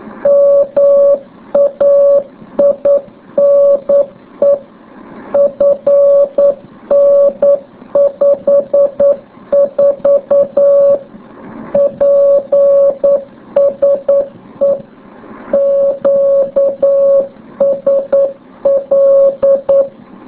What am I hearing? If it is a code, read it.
Morse code, heard as a keyed tone, about 9 wpm, 560 Hz. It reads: MAINE FN54 PSE QSL